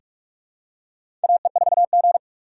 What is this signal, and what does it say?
Morse code, heard as a keyed tone, about 44 wpm, 700 Hz. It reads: AE4G
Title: USA Call Signs - 4 Characters - 2x Character Spacing - 45wpm